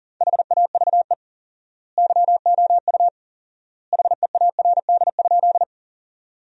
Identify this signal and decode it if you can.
Morse code, heard as a keyed tone, about 37 wpm, 700 Hz. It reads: HAVE YOU HEARD?
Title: 4x Word-Spacing - Sentences from Top 400 Words - 40wpm